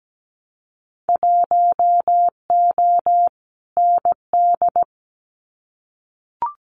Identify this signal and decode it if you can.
Morse code, heard as a keyed tone, about 17 wpm, 700 Hz. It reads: 1O ND
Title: Winter Field Day Exchanges 17wpm